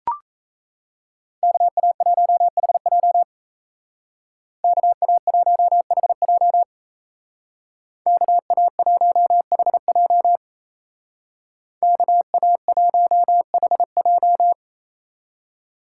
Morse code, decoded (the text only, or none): KA1HJ KA1HJ KA1HJ KA1HJ